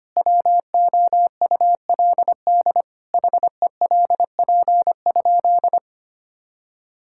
Morse code, heard as a keyed tone, about 25 wpm, 700 Hz. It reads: WOULD HELP?
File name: practice / 25wpm / Sentences with 9-Word Phrases - 25wpm